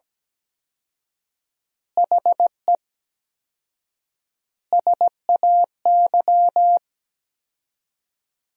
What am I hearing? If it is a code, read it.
Morse code, heard as a keyed tone, about 17 wpm, 700 Hz. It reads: HE SAY
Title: 4x Word-Spacing - Sentences from Top 300 Words - 17wpm